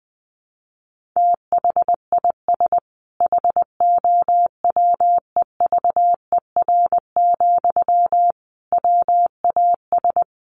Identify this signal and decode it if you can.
Morse code, heard as a keyed tone, about 20 wpm, 700 Hz. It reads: THIS HOWEVER, WAS
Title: Sentences from Top 1400 Words 20wpm